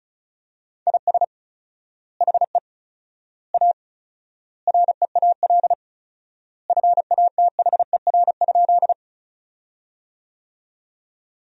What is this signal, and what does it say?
Morse code, heard as a keyed tone, about 34 wpm, 700 Hz. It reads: IS HE A REAL FATHER?
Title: 4x Word-Spacing - Sentences from Top 300 Words - 35wpm